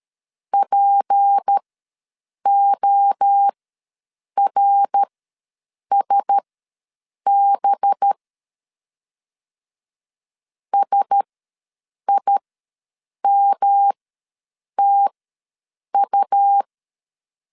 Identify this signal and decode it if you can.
Morse code, heard as a keyed tone, about 13 wpm, spaced as for 7 wpm, 800 Hz. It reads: PORSB SIMTU
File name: LEZIONI / 7 GF9Z / 7 GF ws 12wpm 3140